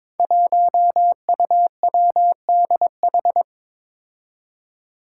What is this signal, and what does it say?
Morse code, heard as a keyed tone, about 23 wpm, 700 Hz. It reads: 1UWDH
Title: Five Challenging Letter-Number Combinations 22wpm